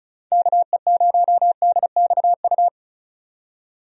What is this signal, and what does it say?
Morse code, heard as a keyed tone, about 34 wpm, 700 Hz. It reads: KE0DXU
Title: USA Call Signs - 6 Characters - 35wpm